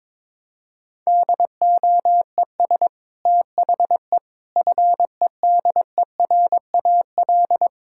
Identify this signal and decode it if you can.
Morse code, heard as a keyed tone, about 22 wpm, 700 Hz. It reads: DOES THE FEDERAL
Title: Sentences from Top 1000 Words - Review Part 5 - 22wpm